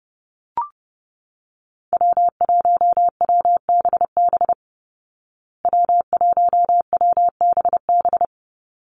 Morse code, heard as a keyed tone, about 30 wpm, 700 Hz. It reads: W1WBB W1WBB
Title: Call Signs Repeated 2x - 30wpm